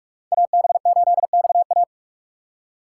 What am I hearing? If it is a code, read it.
Morse code, heard as a keyed tone, about 48 wpm, 700 Hz. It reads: AB8XA
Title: USA Call Signs - 5 Characters - 4x Character Spacing - 45wpm